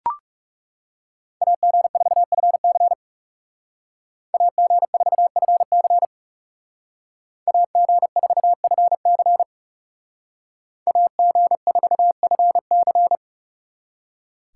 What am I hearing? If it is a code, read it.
Morse code, heard as a keyed tone, about 37 wpm, 700 Hz. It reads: AG4FC AG4FC AG4FC AG4FC
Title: Speed Racing - USA 2x2 Call Signs 30wpm